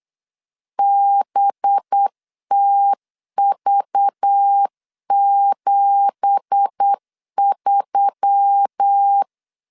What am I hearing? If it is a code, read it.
Morse code, heard as a keyed tone, about 8 wpm, 800 Hz. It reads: BTV73
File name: CW / DET / 1 9 1500C S B 8wpm 5624